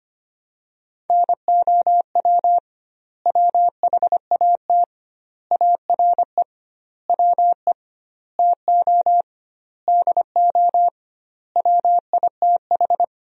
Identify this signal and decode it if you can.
Morse code, heard as a keyed tone, about 25 wpm, 700 Hz. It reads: NOW WHAT ARE WE TO DO WITH
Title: 2x Word-Spacing - Sentences from Top 100 Words - 25wpm